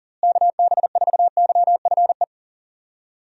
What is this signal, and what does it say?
Morse code, heard as a keyed tone, about 40 wpm, 700 Hz. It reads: KB4YFE
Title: USA Call Signs - 6 Characters - Repeated 2x - 40wpm